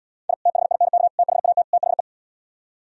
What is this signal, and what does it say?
Morse code, heard as a keyed tone, about 77 wpm, 700 Hz. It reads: I THINK THAT THE